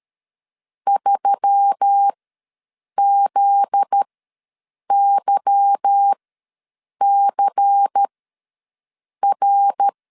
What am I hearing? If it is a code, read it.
Morse code, heard as a keyed tone, about 13 wpm, spaced as for 7 wpm, 800 Hz. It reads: 3ZYCR
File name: LEZIONI / 8 YJ45 / 8 Y ws 12wpm 3409